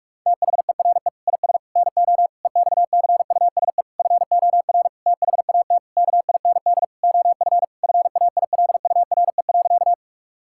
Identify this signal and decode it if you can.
Morse code, heard as a keyed tone, about 46 wpm, 700 Hz. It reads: THERE IS NO EXCUSE FOR THAT KIND OF FAILURE.
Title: Sentences from Top 1000 Words - Review Part 6 - 45wpm